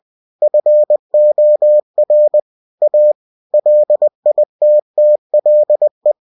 Morse code, heard as a keyed tone, about 20 wpm, 600 Hz. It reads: FOR A LITTLE
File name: cw-books / winnie-the-pooh / winnie-20wpm-page-0079